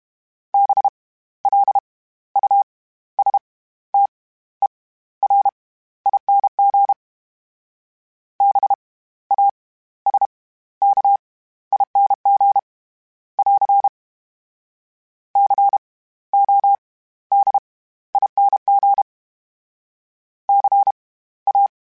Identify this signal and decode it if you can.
Morse code, heard as a keyed tone, about 32 wpm, spaced as for 15 wpm, 800 Hz. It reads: BLUSTERING BASKING <AR> CODING CA
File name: Special INGs / Special INGs-mp3s / SlowToFaster ing @30